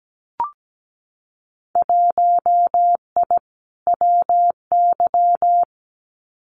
Morse code, heard as a keyed tone, about 17 wpm, 700 Hz. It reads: 1I WY
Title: Winter Field Day Exchanges Encore 17wpm